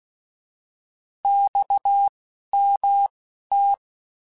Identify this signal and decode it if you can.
Morse code, heard as a keyed tone, about 16 wpm, spaced as for 11 wpm, 800 Hz. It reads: XMT
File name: Abreviations / Abreviations mp3s / Abreviations@10